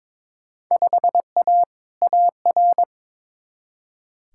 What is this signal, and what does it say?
Morse code, heard as a keyed tone, about 22 wpm, 700 Hz. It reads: 5A AR
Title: Field Day Exchanges Encore - 22wpm